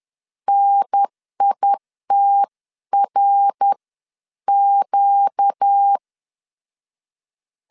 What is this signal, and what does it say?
Morse code, heard as a keyed tone, about 10 wpm, spaced as for 9 wpm, 800 Hz. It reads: NITRQ